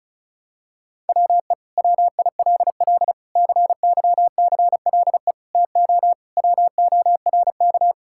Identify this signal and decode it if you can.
Morse code, heard as a keyed tone, about 36 wpm, 700 Hz. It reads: WE WILL CYCLE TO WORK
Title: Sentences from Top 1000 Words - Review Part 6 - 35wpm